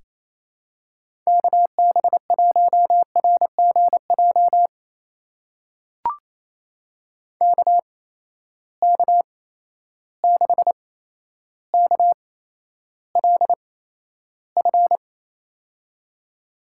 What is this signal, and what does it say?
Morse code, heard as a keyed tone, about 28 wpm, spaced as for 12 wpm, 700 Hz. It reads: KB1RGJ KK6KLF
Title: USA Call Signs - 6 Characters - 8x Character Spacing - 28wpm